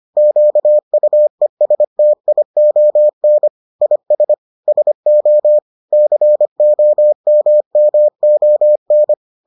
Morse code, heard as a keyed tone, about 25 wpm, 600 Hz. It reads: QUESTION IS SO COMMON